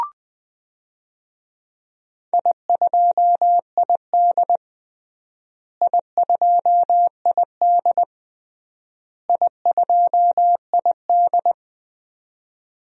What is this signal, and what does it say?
Morse code, heard as a keyed tone, about 20 wpm, 700 Hz. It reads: I2ID I2ID I2ID
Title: DXCC-100 Call Signs - 4 Characters - Repeated 3x - 20wpm